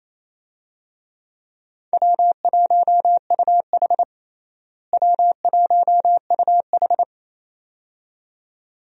Morse code, heard as a keyed tone, about 28 wpm, 700 Hz. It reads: W1UH W1UH
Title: USA Call Signs - 4 Characters - Repeated 2x - 28wpm